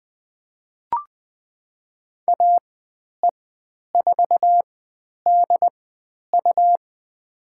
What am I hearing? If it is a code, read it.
Morse code, heard as a keyed tone, about 20 wpm, spaced as for 10 wpm, 700 Hz. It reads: AE4DU
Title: 102 Lesson 17g - Letter u - Call Signs 20wpm